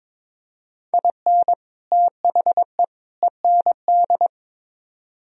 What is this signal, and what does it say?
Morse code, heard as a keyed tone, about 26 wpm, 700 Hz. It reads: IN THE END
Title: Speed-Racing - Sets of 3 Words 22wpm